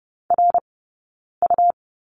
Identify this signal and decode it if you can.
Morse code, heard as a keyed tone, about 30 wpm, spaced as for 9 wpm, 690 Hz. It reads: RU
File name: Lesson 17d - Letter u - Two Characters 30wpm